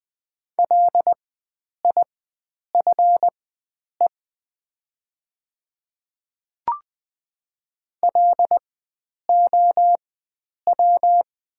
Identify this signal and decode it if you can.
Morse code, heard as a keyed tone, about 20 wpm, spaced as for 9 wpm, 700 Hz. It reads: LIFE LOW
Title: 4x Character Spacing - Top 300 Words - 20wpm